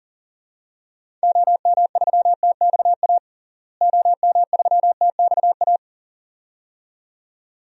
Morse code, heard as a keyed tone, about 40 wpm, 700 Hz. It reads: OM3TXA OM3TXA
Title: DXCC-100 Call Signs - 6 Characters - Repeated 2x - 40wpm